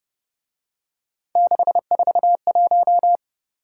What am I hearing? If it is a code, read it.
Morse code, heard as a keyed tone, about 29 wpm, 700 Hz. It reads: 641
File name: Speed-Racing - Three Letter-Number Combinations - 25wpm